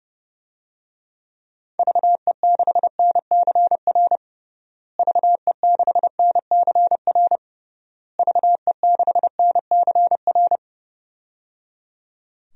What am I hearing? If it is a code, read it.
Morse code, heard as a keyed tone, about 30 wpm, 700 Hz. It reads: VE6NCR VE6NCR VE6NCR
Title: DXCC-100 Call Signs - 6 Characters - Repeated 3x - 30wpm